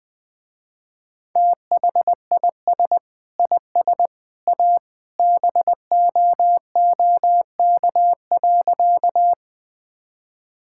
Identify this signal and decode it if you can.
Morse code, heard as a keyed tone, about 20 wpm, 700 Hz. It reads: THIS IS A BOOK.
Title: Sentences with 3-Word Phrases - Spoken Beginning - 20wpm